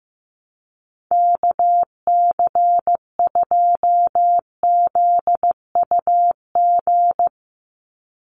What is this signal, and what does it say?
Morse code, heard as a keyed tone, about 15 wpm, 700 Hz. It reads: KC2ZUG
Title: USA Call Signs 15wpm